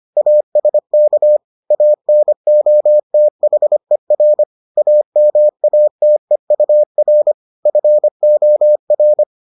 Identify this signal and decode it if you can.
Morse code, heard as a keyed tone, about 25 wpm, 600 Hz. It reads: ASK ANOTHER AMATEUR FOR